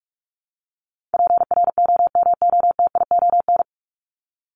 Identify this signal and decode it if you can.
Morse code, heard as a keyed tone, about 44 wpm, 700 Hz. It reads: PROMOTION
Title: Top 1501-1750 Words 45wpm